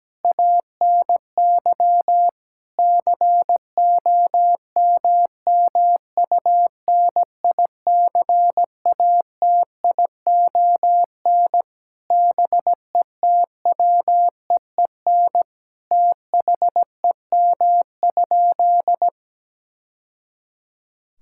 Morse code, heard as a keyed tone, about 17 wpm, 700 Hz. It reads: ANY COMMUNICATION BETWEEN THEM?
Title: Sentences from Top 1000 Words - Review Part 6 - 17wpm